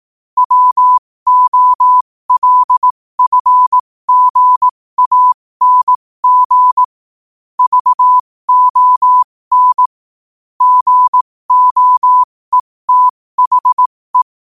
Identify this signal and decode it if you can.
Morse code, heard as a keyed tone, about 18 wpm, spaced as for 15 wpm, 1000 Hz. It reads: WOLFGANG VON GOETHE
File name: qotdfiles / 15wpm